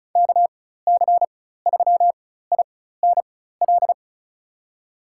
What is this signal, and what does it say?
Morse code, heard as a keyed tone, about 36 wpm, spaced as for 16 wpm, 700 Hz. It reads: KC3INL
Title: USA Call Signs - 4x Character Spacing - 35wpm